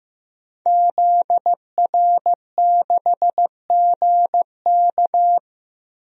Morse code, heard as a keyed tone, about 15 wpm, 700 Hz. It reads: ZR6GK
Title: DXCC-100 Call Signs - 5 Characters - Repeated 3x - 15wpm